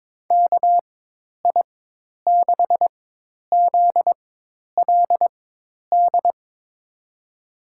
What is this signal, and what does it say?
Morse code, heard as a keyed tone, about 21 wpm, spaced as for 10 wpm, 700 Hz. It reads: KI6ZLD
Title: USA Call Signs - 6 Characters - 4x Character Spacing - 22wpm